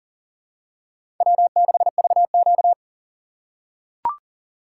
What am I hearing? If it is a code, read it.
Morse code, heard as a keyed tone, about 40 wpm, 700 Hz. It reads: W6VQ
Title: USA Call Signs - 4 Characters - 3x Character Spacing - 40wpm